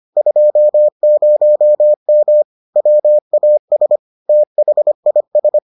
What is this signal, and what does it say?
Morse code, heard as a keyed tone, about 25 wpm, 600 Hz. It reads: 20M WAS THIS